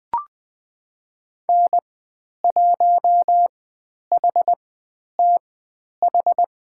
Morse code, heard as a keyed tone, about 20 wpm, spaced as for 10 wpm, 700 Hz. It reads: N1HTH
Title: 109 Lesson 18g - Letter m - Call Signs 20wpm